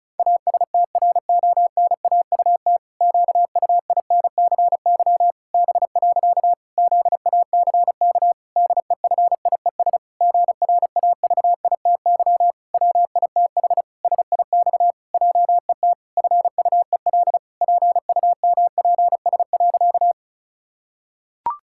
Morse code, heard as a keyed tone, about 36 wpm, 700 Hz. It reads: ASTRONAUT QUINCY B. ZACK DEFIES GRAVITY WITH SIX JET FUEL PUMPS.
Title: Pangram Challenge v2 - 35wpm